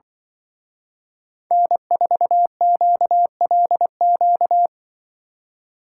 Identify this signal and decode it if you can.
Morse code, heard as a keyed tone, about 24 wpm, 700 Hz. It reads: N4QLQ
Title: Speed Racing - USA 1x3 Call Signs 20wpm